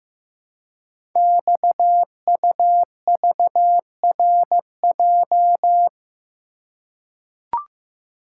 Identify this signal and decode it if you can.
Morse code, heard as a keyed tone, about 15 wpm, 700 Hz. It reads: XUVRJ
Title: Code Groups - 5 Letters - 15wpm